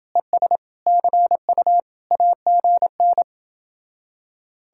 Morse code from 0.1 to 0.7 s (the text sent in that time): ES